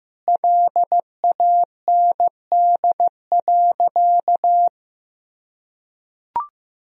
Morse code, heard as a keyed tone, about 15 wpm, 700 Hz. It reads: LAND.
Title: Sentences from Top 1000 Words - Review Part 1 - 15wpm